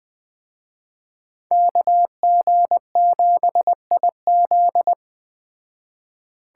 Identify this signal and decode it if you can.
Morse code, heard as a keyed tone, about 21 wpm, 700 Hz. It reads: KG7IZ
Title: Speed Racing - USA 2x2 Call Signs 17wpm